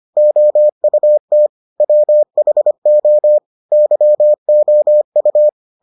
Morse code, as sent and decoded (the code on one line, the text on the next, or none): --- ..- - / .-- .... --- / -.-- --- ..-
OUT WHO YOU